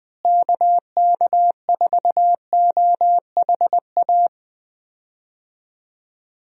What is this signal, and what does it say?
Morse code, heard as a keyed tone, about 21 wpm, 700 Hz. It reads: KK4OHA
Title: USA Call Signs - 6 Characters - Repeated 3x - 20wpm